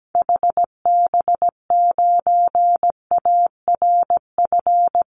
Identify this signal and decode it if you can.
Morse code, heard as a keyed tone, about 17 wpm, 700 Hz. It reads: HB9ARF
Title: Call Signs Repeated 3x - 17wpm